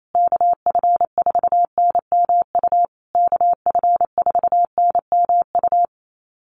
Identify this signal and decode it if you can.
Morse code, heard as a keyed tone, about 28 wpm, 700 Hz. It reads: KF4NMU KF4NMU
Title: USA Call Signs Repeated 3x 28wpm